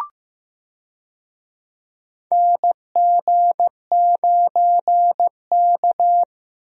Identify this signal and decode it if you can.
Morse code, heard as a keyed tone, about 15 wpm, 700 Hz. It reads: NG9K
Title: USA Call Signs - 4 Characters - Repeated 2x - 15wpm